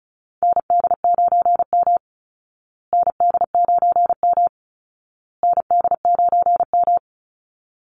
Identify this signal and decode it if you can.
Morse code, heard as a keyed tone, about 34 wpm, 700 Hz. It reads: ND9M ND9M ND9M
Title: Call Signs Repeated 2x - 35wpm